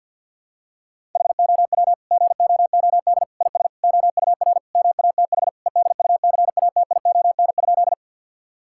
Morse code, heard as a keyed tone, about 52 wpm, 700 Hz. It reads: HOW GOOD IS OUR MATH EDUCATION?
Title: Sentences from Top 1000 Words - Review Part 6 - 50wpm